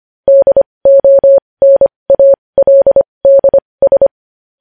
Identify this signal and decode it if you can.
Morse code, heard as a keyed tone, about 25 wpm, spaced as for 20 wpm, 550 Hz. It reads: DONALDS
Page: news headlines in morse code at 20 wpm